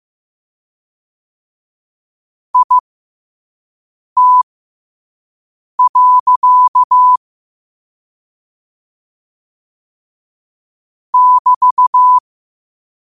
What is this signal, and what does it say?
Morse code, heard as a keyed tone, about 15 wpm, spaced as for 5 wpm, 1000 Hz. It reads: IT. <BT>
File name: qotdfiles / 5wpm